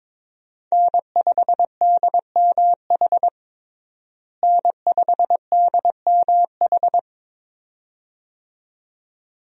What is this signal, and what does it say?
Morse code, heard as a keyed tone, about 22 wpm, 700 Hz. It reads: N5DMH N5DMH
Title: USA Call Signs - 5 Characters - Repeated 2x - 22wpm